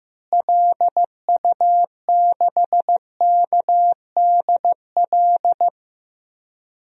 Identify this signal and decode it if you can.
Morse code, heard as a keyed tone, about 15 wpm, 700 Hz. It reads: LU6KDL
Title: DXCC-100 Call Signs - 6 Characters - Repeated 3x - 15wpm